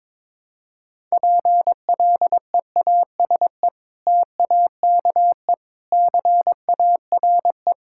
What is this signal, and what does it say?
Morse code, heard as a keyed tone, about 22 wpm, 700 Hz. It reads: PLEASE TAKE CARE
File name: practice / 22wpm / Sentences with 3-Word Phrases - 22wpm